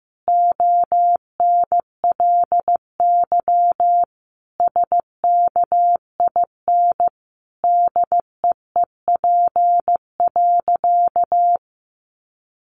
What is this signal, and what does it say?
Morse code, heard as a keyed tone, about 15 wpm, 700 Hz. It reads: ONLY SKIN DEEP.
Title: Sentences from Top 1300 Words 15wpm